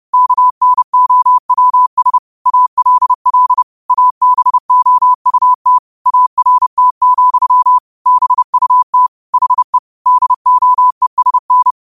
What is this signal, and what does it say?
Morse code, heard as a keyed tone, about 30 wpm, 1000 Hz. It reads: KNOWS ALL ABOUT ART, BUT HE DOESN
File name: qotdfiles / 30wpm